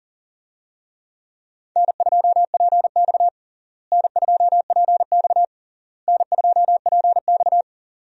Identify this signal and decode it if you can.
Morse code, heard as a keyed tone, about 40 wpm, 700 Hz. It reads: N2PX N2PX N2PX
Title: USA Call Signs - 4 Characters - Repeated 3x - 40wpm